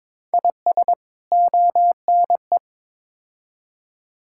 Morse code, heard as a keyed tone, about 22 wpm, 700 Hz. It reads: IS ONE
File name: practice / 22wpm / Build A Phrase - 5-Word Phrases - 22wpm